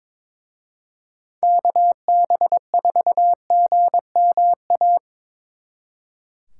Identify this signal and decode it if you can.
Morse code, heard as a keyed tone, about 24 wpm, 700 Hz. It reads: KB4GMA
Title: Speed Racing - USA 2x3 Call Signs 22wpm